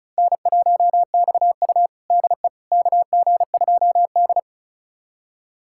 Morse code, heard as a keyed tone, about 36 wpm, 700 Hz. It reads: N1XU DE KG2D
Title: USA Callsign DE Callsign - 4 Characters - 35wpm